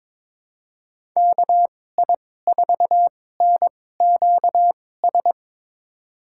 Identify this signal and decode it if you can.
Morse code, heard as a keyed tone, about 22 wpm, spaced as for 16 wpm, 700 Hz. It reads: KI4NQS
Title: USA Call Signs - 6 Characters - 2x Character Spacing - 22wpm